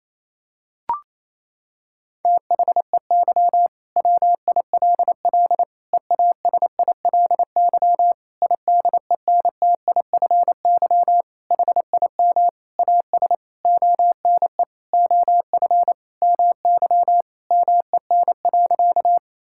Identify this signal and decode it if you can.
Morse code, heard as a keyed tone, about 28 wpm, 700 Hz. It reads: THEY WILL EASILY IDENTIFY HIM AS ONE OF MY MEN.
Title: Sentences from Top 1000 Words - Review Part 6 - 28wpm